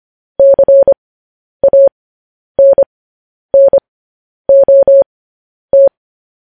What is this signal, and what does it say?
Morse code, heard as a keyed tone, about 25 wpm, spaced as for 10 wpm, 550 Hz. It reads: CANNOT
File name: podcast / News headlines-1x-25x10